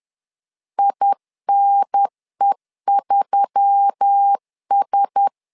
Morse code, heard as a keyed tone, about 10 wpm, 800 Hz. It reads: INE3S
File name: LEZIONI / 5 LBVC / 5 LBV 10wpm 2248